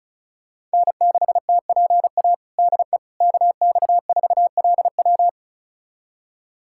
Morse code, heard as a keyed tone, about 36 wpm, 700 Hz. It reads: N6TPA DE KX4LW
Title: USA Callsign DE Callsign - 5 Characters - 35wpm